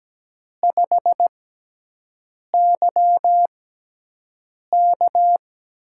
Morse code, heard as a keyed tone, about 17 wpm, spaced as for 6 wpm, 700 Hz. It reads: 5YK
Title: USA Call Signs - 4 Characters - 6x Character Spacing - 17wpm